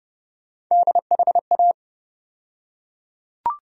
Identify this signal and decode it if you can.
Morse code, heard as a keyed tone, about 30 wpm, 700 Hz. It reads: DHA
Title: USA Call Sign Postfixes - 3 Letters 30wpm